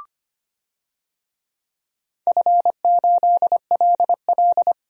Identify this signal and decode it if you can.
Morse code, heard as a keyed tone, about 25 wpm, 700 Hz. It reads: F8LL
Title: DXCC-100 Call Signs - 4 Characters - 25wpm